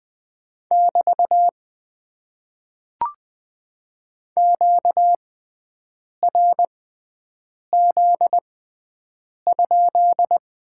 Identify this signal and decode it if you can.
Morse code, heard as a keyed tone, about 20 wpm, spaced as for 7 wpm, 700 Hz. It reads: <BT> QRZ?
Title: Top 90 QSO Elements - 6x Character Spacing - 20wpm